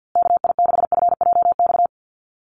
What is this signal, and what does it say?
Morse code, heard as a keyed tone, about 49 wpm, 700 Hz. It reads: KI6FJX
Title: USA Call Signs Repeated 3x 50wpm